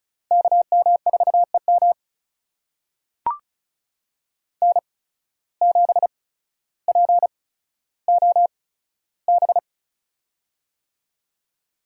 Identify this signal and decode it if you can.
Morse code, heard as a keyed tone, about 36 wpm, spaced as for 15 wpm, 700 Hz. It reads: KM4EM N7POB
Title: USA Call Signs - 5 Characters - 8x Character Spacing - 35wpm